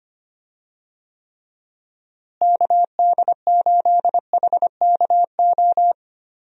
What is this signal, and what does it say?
Morse code, heard as a keyed tone, about 25 wpm, 700 Hz. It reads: KD8HKO
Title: USA Call Signs - 6 Characters - Repeated 3x - 25wpm